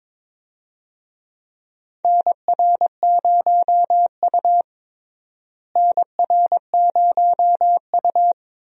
Morse code, heard as a keyed tone, about 22 wpm, 700 Hz. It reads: NR0U NR0U
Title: USA Call Signs - 4 Characters - Repeated 3x - 22wpm